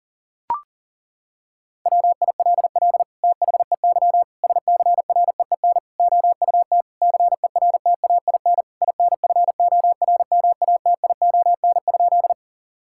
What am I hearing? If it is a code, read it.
Morse code, heard as a keyed tone, about 40 wpm, 700 Hz. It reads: WILL THEY SCREEN OUT CERTAIN INFORMATION?
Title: Sentences from Top 1000 Words - Review Part 6 - 40wpm